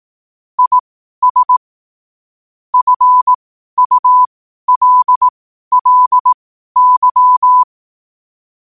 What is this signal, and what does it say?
Morse code, heard as a keyed tone, about 18 wpm, spaced as for 12 wpm, 1000 Hz. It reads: IS FULLY
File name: qotdfiles / 12wpm